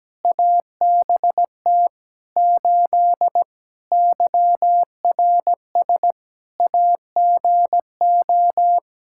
Morse code, heard as a keyed tone, about 17 wpm, 700 Hz. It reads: ABT 8 YRS AGO
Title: Practice QSO - Exchange 3 - More Info - 17wpm